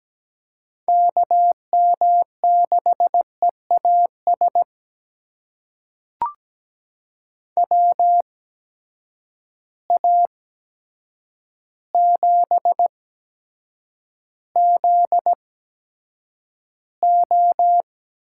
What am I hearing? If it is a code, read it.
Morse code, heard as a keyed tone, about 17 wpm, spaced as for 8 wpm, 700 Hz. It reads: KM6EAS WA7ZO
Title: USA Call Signs - 6 Characters - 8x Character Spacing - 17wpm